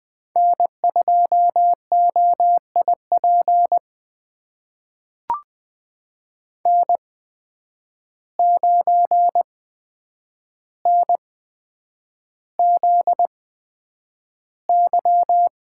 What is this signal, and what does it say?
Morse code, heard as a keyed tone, about 20 wpm, spaced as for 9 wpm, 700 Hz. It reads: N2OIP N9NZY
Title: USA Call Signs - 5 Characters - 8x Character Spacing - 20wpm